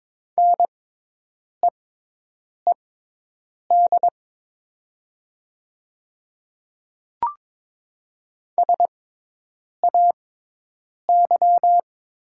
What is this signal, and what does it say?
Morse code, heard as a keyed tone, about 22 wpm, spaced as for 8 wpm, 700 Hz. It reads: NEED SAY
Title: 6x Character Spacing - Top 300 Words - 22wpm